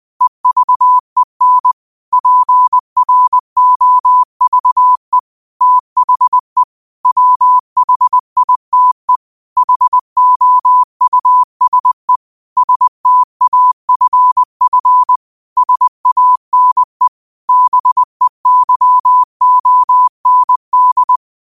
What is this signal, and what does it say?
Morse code, heard as a keyed tone, about 20 wpm, 1000 Hz. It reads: EVEN PROVE THE WHITE HOUSE STAFF SANE BEYOND